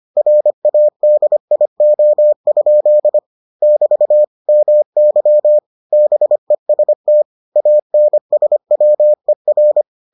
Morse code, heard as a keyed tone, about 25 wpm, 600 Hz. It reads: RADIO? <BT> MY BEST ANSWER